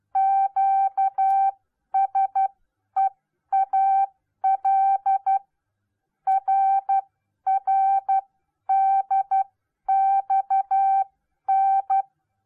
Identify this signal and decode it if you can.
Morse code, heard as a keyed tone, about 12 wpm, spaced as for 10 wpm, 780 Hz. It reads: QSEAL RRDXN